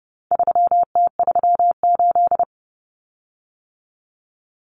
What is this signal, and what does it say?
Morse code, heard as a keyed tone, about 30 wpm, 700 Hz. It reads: 3T38
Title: Common Cut Numbers - 4-Digit 30wpm